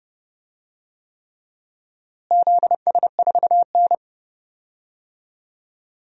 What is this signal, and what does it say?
Morse code, heard as a keyed tone, about 30 wpm, 700 Hz. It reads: ZS4N
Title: DXCC-100 Call Signs - 4 Characters - Repeated 3x - 30wpm